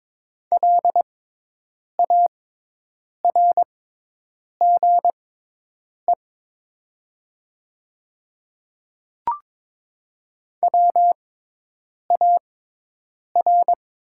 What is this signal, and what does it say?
Morse code, heard as a keyed tone, about 22 wpm, spaced as for 8 wpm, 700 Hz. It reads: LARGE WAR
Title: 6x Character Spacing - Top 500 Words - 22wpm